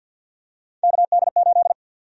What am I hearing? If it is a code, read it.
Morse code, heard as a keyed tone, about 50 wpm, 700 Hz. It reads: KD8
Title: USA Call Sign Prefixes - 50wpm